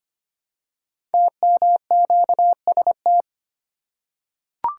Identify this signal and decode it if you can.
Morse code, heard as a keyed tone, about 25 wpm, 700 Hz. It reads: TMQST
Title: Code Groups - 5 Letter:Numbers - 25wpm